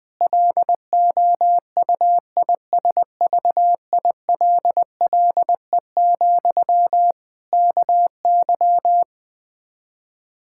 Morse code, heard as a keyed tone, about 20 wpm, 700 Hz. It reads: LOUISVILLE, KY
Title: K1USN SST QSO Practice 20wpm